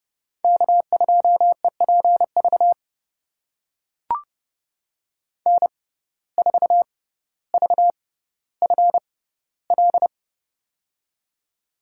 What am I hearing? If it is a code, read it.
Morse code, heard as a keyed tone, about 31 wpm, spaced as for 15 wpm, 700 Hz. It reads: K2EPV N4VFL
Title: USA Call Signs - 5 Characters - 6x Character Spacing - 30wpm